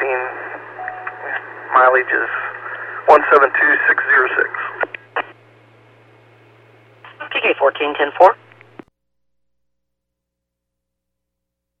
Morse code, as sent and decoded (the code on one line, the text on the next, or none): none